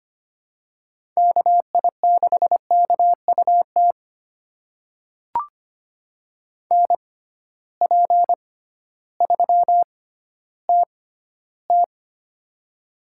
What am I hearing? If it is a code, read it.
Morse code, heard as a keyed tone, about 25 wpm, spaced as for 14 wpm, 700 Hz. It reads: KI6KUT NP3TT
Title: USA Call Signs - 6x Character Spacing - 25wpm